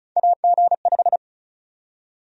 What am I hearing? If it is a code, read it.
Morse code, heard as a keyed tone, about 36 wpm, 700 Hz. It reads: AG5
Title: USA Call Sign Prefixes - Repeated 2x - 35wpm